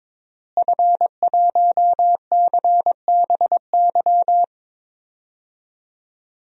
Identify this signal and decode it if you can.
Morse code, heard as a keyed tone, about 22 wpm, 700 Hz. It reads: F1CBY
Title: DXCC-100 Call Signs - 5 Characters - Repeated 3x - 22wpm